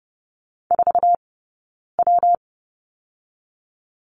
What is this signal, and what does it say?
Morse code, heard as a keyed tone, about 30 wpm, 700 Hz. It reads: 4 W
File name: Lesson 19d - Letter w - Two Characters 30wpm